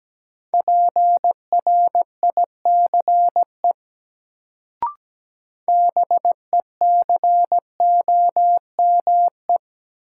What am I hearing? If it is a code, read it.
Morse code, heard as a keyed tone, about 17 wpm, 700 Hz. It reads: PRICE BECOME
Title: Sentences from Top 600 Words - New Word Drill - 17wpm